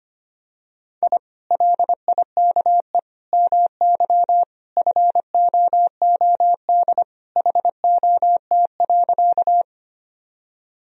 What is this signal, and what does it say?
Morse code, heard as a keyed tone, about 25 wpm, 700 Hz. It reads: I LIKE MY FOOD HOT.
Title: Sentences from Top 1000 Words - Review Part 3 - 25wpm